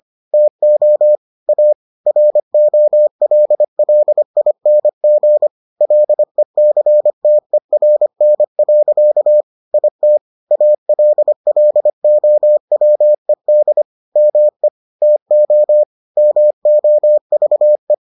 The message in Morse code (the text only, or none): TO A ROLLING LECTERN. IT ALLOWED ME TO MOVE